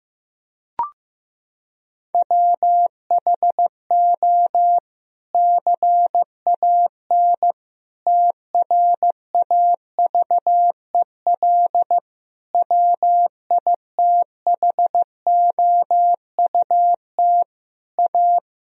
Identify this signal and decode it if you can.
Morse code, heard as a keyed tone, about 15 wpm, 700 Hz. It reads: WHO CAN TRAVEL WITHOUT A
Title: Sentences from Top 1000 Words - Review Part 3 - 15wpm